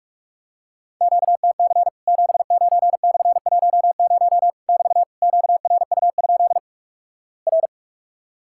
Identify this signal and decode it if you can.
Morse code, heard as a keyed tone, about 46 wpm, 700 Hz. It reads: QTC 79/10 <BT> QRU? R
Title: Worked All Europe DX Contest - QTC Exchange - 45wpm